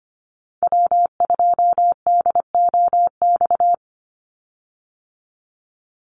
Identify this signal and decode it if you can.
Morse code, heard as a keyed tone, about 25 wpm, 700 Hz. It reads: W2DOX
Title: USA Call Signs 25wpm